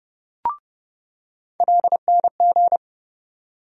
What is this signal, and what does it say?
Morse code, heard as a keyed tone, about 30 wpm, 700 Hz. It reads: LNG
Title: USA Call Sign Postfixes - 3 Letters 30wpm